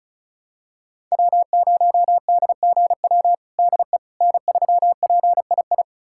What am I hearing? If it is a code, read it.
Morse code, heard as a keyed tone, about 36 wpm, 700 Hz. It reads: W0DGW DE N3PII
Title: USA Callsign DE Callsign - 5 Characters - 35wpm